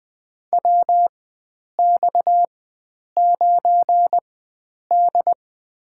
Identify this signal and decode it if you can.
Morse code, heard as a keyed tone, about 21 wpm, spaced as for 9 wpm, 700 Hz. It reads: WX9D
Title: USA Call Signs - 4 Characters - 4x Character Spacing - 20wpm